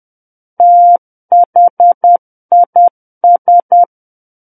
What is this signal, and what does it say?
Morse code, heard as a keyed tone, about 10 wpm, 700 Hz. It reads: THIS
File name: Book One / 01-Chapter 1 - 10 WPM